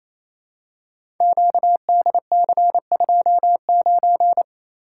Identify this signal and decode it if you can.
Morse code, heard as a keyed tone, about 28 wpm, 700 Hz. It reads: QDC29
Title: Code Groups - 5 Letter:Numbers - 28wpm